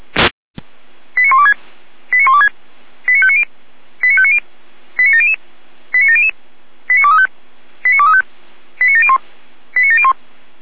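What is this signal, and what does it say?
Five-tone selective call (ZVEI), heard as quick runs of five tones, tones of about 70 ms. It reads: 89126 89126 89590 89590 89700 89700 89235 89235 89701 89701
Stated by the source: ZVEI five-tone selective call received on an SDR.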